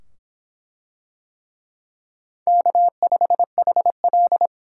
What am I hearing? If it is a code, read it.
Morse code, heard as a keyed tone, about 24 wpm, 700 Hz. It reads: K5HL